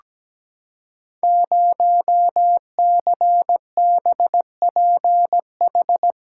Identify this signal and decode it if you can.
Morse code, heard as a keyed tone, about 17 wpm, 700 Hz. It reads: 0CBPH
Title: Code Groups - 5 Letter:Numbers - 17wpm